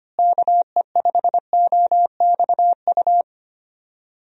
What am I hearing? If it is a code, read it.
Morse code, heard as a keyed tone, about 25 wpm, 700 Hz. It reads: KE5OXU
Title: USA Call Signs - 6 Characters - Repeated 3x - 25wpm